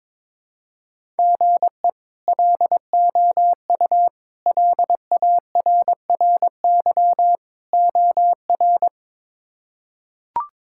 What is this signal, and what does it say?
Morse code, heard as a keyed tone, about 22 wpm, 700 Hz. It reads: GE LOU LARRY OR
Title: K1USN SST QSO Practice - 22wpm